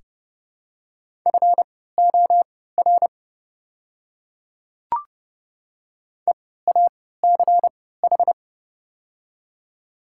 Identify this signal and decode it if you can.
Morse code, heard as a keyed tone, about 31 wpm, spaced as for 17 wpm, 700 Hz. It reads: FOR EACH
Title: 3x Character Spacing - Top 100 Words - 30wpm